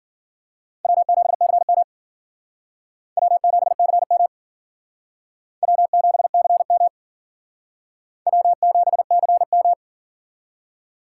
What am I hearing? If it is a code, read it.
Morse code, heard as a keyed tone, about 51 wpm, 700 Hz. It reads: W7CM W7CM W7CM W7CM